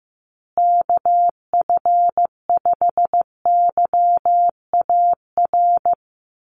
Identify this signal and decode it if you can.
Morse code, heard as a keyed tone, about 14 wpm, 700 Hz. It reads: KF5YAR
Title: USA Call Signs Repeated 3x 15wpm